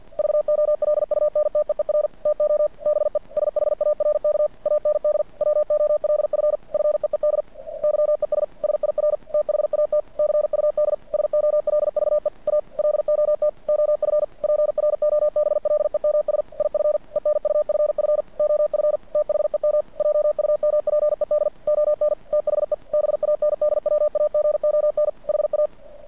Morse code, heard as a keyed tone, about 50 wpm, 590 Hz. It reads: VOLUNTEER TO BE SHARK AND WOLF FEED YES SIR THAT CUD SOLVE A LOT OF PROBLEMS IF ENUFF OF THEM JUMPED ON THE BANDWAGON HA
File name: archived / CRSnet